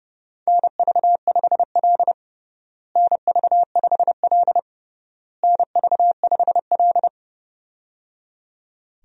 Morse code, heard as a keyed tone, about 29 wpm, 700 Hz. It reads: NV5L NV5L NV5L
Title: USA Call Signs - 4 Characters - Repeated 3x - 30wpm